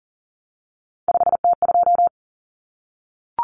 Morse code, heard as a keyed tone, about 40 wpm, 700 Hz. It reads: 5T2
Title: Common Cut Numbers - 3-Digit 40wpm